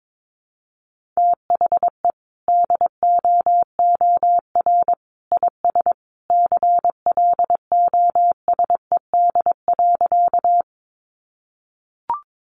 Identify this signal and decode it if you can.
Morse code, heard as a keyed tone, about 22 wpm, 700 Hz. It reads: THE DOOR IS CLOSED.
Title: Sentences from Top 1400 Words 22wpm